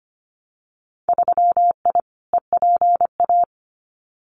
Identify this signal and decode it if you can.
Morse code, heard as a keyed tone, about 25 wpm, 700 Hz. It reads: 3I EPA
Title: Winter Field Day Exchanges 25wpm